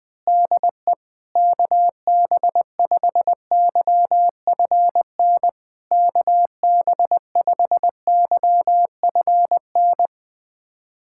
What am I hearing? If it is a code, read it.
Morse code, heard as a keyed tone, about 20 wpm, 700 Hz. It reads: DE KB5YFN KB5YFN
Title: SOTA QSO Practice 20wpm